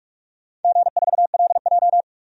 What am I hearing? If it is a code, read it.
Morse code, heard as a keyed tone, about 41 wpm, 700 Hz. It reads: G4LJ
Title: DXCC-100 Call Signs - 4 Characters - 45wpm